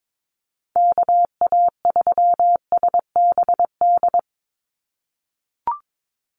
Morse code, heard as a keyed tone, about 22 wpm, 700 Hz. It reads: KA3SBD
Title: USA Call Signs 22wpm